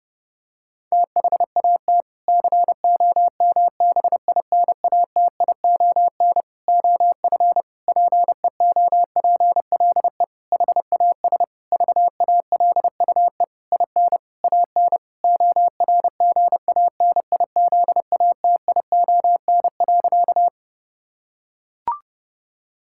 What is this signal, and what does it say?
Morse code, heard as a keyed tone, about 30 wpm, 700 Hz. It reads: THAT COMBINATION OF PEOPLE HAS VALUE IN AN ORGANIZATION.
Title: Sentences from Top 1000 Words - Review Part 6 - 30wpm